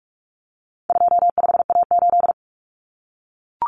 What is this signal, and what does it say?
Morse code, heard as a keyed tone, about 44 wpm, 700 Hz. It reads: 25A8
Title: Common Cut Numbers - 4-Digit 45wpm